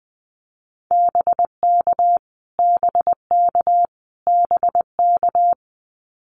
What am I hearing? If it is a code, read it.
Morse code, heard as a keyed tone, about 20 wpm, 700 Hz. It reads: BK BK BK
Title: Abbreviations Repeated 3x - 20wpm